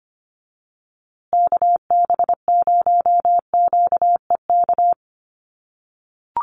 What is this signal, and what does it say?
Morse code, heard as a keyed tone, about 25 wpm, 700 Hz. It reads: KB0QEK
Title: USA Call Signs Repeated 2x 25wpm